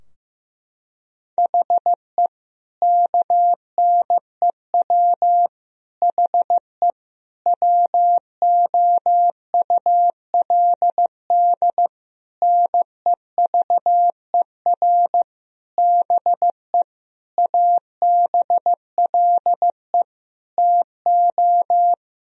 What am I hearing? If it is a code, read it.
Morse code, heard as a keyed tone, about 15 wpm, 700 Hz. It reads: HE KNEW HE WOULD NEVER BE ABLE TO
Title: Sentences from Top 1000 Words - Review Part 2 - 15wpm